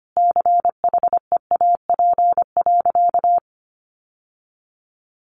Morse code, heard as a keyed tone, about 25 wpm, 700 Hz. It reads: CHEAP.
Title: Sentences from Top 1200 Words 25wpm